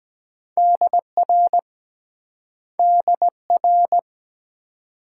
Morse code, heard as a keyed tone, about 23 wpm, 700 Hz. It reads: DR DR